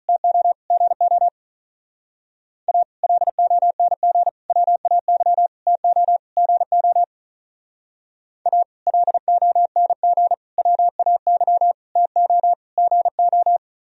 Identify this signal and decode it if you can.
Morse code, heard as a keyed tone, about 45 wpm, 700 Hz. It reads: TO GO A LONG WAY TO GO A LONG WAY TO GO